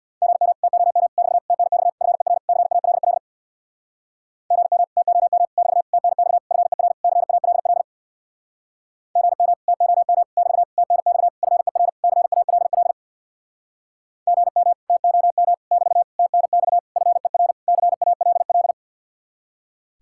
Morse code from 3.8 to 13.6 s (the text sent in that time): GM TOM <BT> TNX FER CALL GM TOM <BT> TNX FER CALL